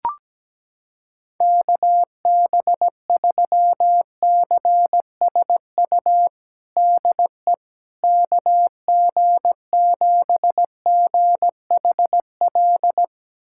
Morse code, heard as a keyed tone, about 17 wpm, 700 Hz. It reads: KB3CSU DE KG7GHL
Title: USA Callsign DE Callsign - 6 Characters - 17wpm